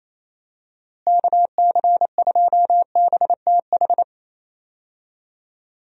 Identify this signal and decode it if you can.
Morse code, heard as a keyed tone, about 28 wpm, 700 Hz. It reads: KC2BTH
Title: USA Call Signs - 6 Characters - Repeated 2x - 28wpm